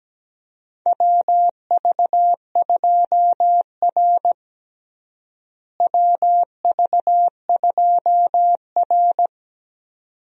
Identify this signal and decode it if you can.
Morse code, heard as a keyed tone, about 17 wpm, 700 Hz. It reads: WV2R WV2R
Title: USA Call Signs - 4 Characters - Repeated 2x - 17wpm